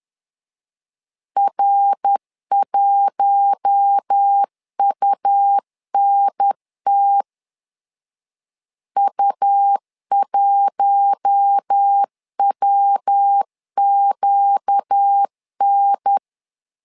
Morse code, heard as a keyed tone, about 10 wpm, 800 Hz. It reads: R1UNT U1WQN
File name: LEZIONI / 4 Q1 10wpm 2448